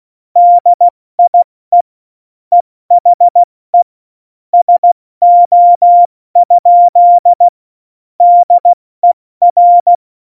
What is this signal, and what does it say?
Morse code, heard as a keyed tone, about 16 wpm, 700 Hz. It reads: DIE EHE SO? DER
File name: Morsekurs / Witze / Witz 02. Juli 2019, 70 F80